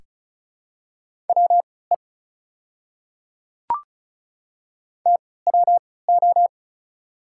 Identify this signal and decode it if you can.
Morse code, heard as a keyed tone, about 36 wpm, spaced as for 20 wpm, 700 Hz. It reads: WE TWO
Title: 3x Character Spacing - Top 200 Words - 35wpm